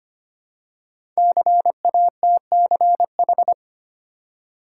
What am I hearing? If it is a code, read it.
Morse code, heard as a keyed tone, about 25 wpm, 700 Hz. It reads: CATCH
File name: 5 Letter Words - First 3 Letters Given - 25wpm